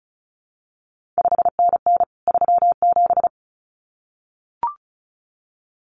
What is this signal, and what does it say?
Morse code, heard as a keyed tone, about 36 wpm, 700 Hz. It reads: 5NN 37
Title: CQ WW DX Contest Exchange Encore - 35wpm